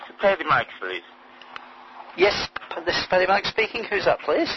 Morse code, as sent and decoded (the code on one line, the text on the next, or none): none